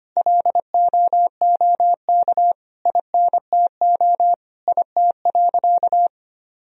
Morse code, heard as a keyed tone, about 25 wpm, 700 Hz. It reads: LOOK INTO IT.
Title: Sentences from Top 1000 Words - Review Part 1 - 25wpm